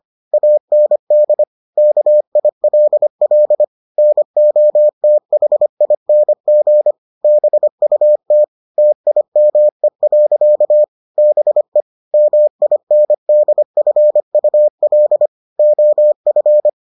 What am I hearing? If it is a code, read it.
Morse code, heard as a keyed tone, about 25 wpm, 600 Hz. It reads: AND KILL NOTHING BUT TIME. BE MINDFUL OF